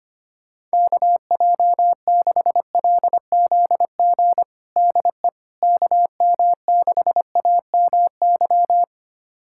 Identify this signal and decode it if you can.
Morse code, heard as a keyed tone, about 25 wpm, 700 Hz. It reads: KJ6LZG DE KM6AMY
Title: USA Callsign DE Callsign - 6 Characters - 25wpm